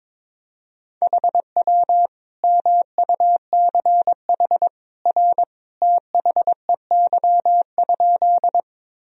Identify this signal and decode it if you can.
Morse code, heard as a keyed tone, about 22 wpm, 700 Hz. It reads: HW MUCH R THEY?
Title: Sentences from Top 200 Words and Abbreviations 22wpm